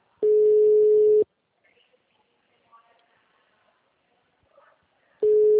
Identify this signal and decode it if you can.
Ringback tone (Europe, 425 Hz), with speech.